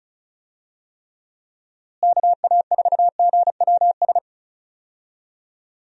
Morse code, heard as a keyed tone, about 34 wpm, 700 Hz. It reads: KA4GWS
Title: USA Call Signs - 6 Characters - Repeated 2x - 35wpm